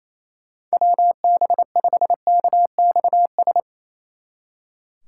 Morse code, heard as a keyed tone, about 34 wpm, 700 Hz. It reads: WB5KXS
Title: Speed Racing - USA Call Signs 28wpm